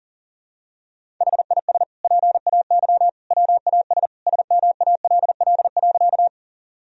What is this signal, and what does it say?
Morse code, heard as a keyed tone, about 40 wpm, 700 Hz. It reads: HIS PAY WAS SMALL.
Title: Sentences from Top 1000 Words - Review Part 1 - 40wpm